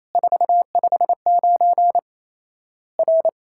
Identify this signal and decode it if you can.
Morse code, heard as a keyed tone, about 28 wpm, 700 Hz. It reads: 459 R
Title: Worked All Europe DX Contest - QTC Exchange - 28wpm